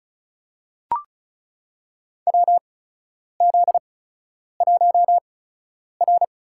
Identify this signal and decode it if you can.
Morse code, heard as a keyed tone, about 34 wpm, spaced as for 10 wpm, 700 Hz. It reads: WZ1R
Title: USA Call Signs - 4 Characters - 8x Character Spacing - 35wpm